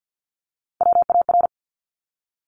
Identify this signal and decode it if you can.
Morse code, heard as a keyed tone, about 50 wpm, 700 Hz. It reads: WAR
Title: Top Three Letter Combinations in Words 50wpm